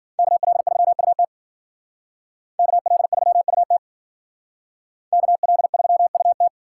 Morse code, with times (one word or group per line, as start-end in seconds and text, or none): 0.2-1.2 KL3UT
2.6-3.8 KL3UT
5.1-6.5 KL3UT